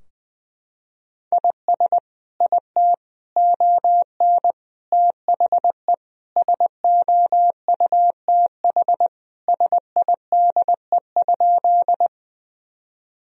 Morse code, heard as a keyed tone, about 20 wpm, 700 Hz. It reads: IS IT ON THE SOUTH SIDE?
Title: Sentences from Top 1000 Words - Review Part 2 - 20wpm